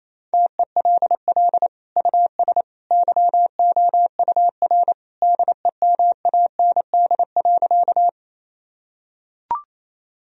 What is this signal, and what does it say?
Morse code, heard as a keyed tone, about 28 wpm, 700 Hz. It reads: TELL US YOUR DEMAND.
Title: Sentences from Top 1000 Words - Review Part 4 - 28wpm